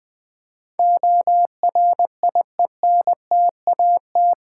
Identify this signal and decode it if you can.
Morse code, heard as a keyed tone, about 20 wpm, 700 Hz. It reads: ORIENTAT
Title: Build A Word - 4 Letters At A Time - 11-Letter Words - 20wpm